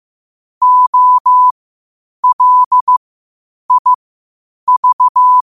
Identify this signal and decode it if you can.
Morse code, heard as a keyed tone, about 15 wpm, spaced as for 8 wpm, 1000 Hz. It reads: OLIV
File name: qotdfiles / 8wpm